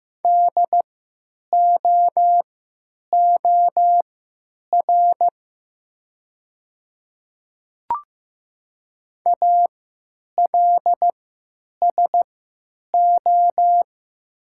Morse code, heard as a keyed tone, about 15 wpm, spaced as for 9 wpm, 700 Hz. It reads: DOOR ALSO
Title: 3x Character Spacing - Top 300 Words - 15wpm